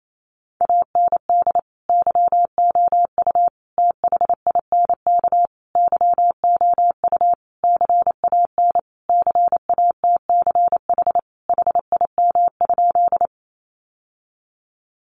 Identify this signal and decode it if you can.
Morse code, heard as a keyed tone, about 28 wpm, 700 Hz. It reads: AND YOU THINK YOU CAN CATCH HIM?
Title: Sentences from Top 1500 Words 28wpm